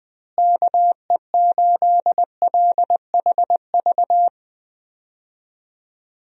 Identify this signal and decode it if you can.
Morse code, heard as a keyed tone, about 20 wpm, 700 Hz. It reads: KE8LHV
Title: USA Call Signs - 6 Characters - Repeated 3x - 20wpm